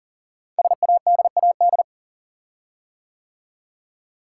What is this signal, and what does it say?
Morse code, heard as a keyed tone, about 38 wpm, 690 Hz. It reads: SADAD
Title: Code Groups - 5 Letters - 40wpm